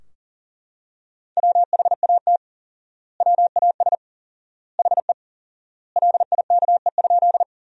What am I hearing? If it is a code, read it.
Morse code, heard as a keyed tone, about 40 wpm, 700 Hz. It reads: WHAT WAS HE LIKE?
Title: 4x Word-Spacing - Sentences from Top 100 Words - 40wpm